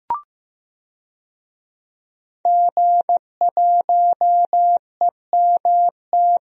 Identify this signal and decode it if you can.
Morse code, heard as a keyed tone, about 16 wpm, 700 Hz. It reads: G1EMT
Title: DXCC-100 Call Signs - 5 Characters - Repeated 2x - 15wpm